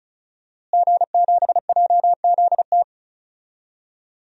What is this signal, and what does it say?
Morse code, heard as a keyed tone, about 34 wpm, 700 Hz. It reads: G7JZT
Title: DXCC-100 Call Signs - 5 Characters - Repeated 2x - 35wpm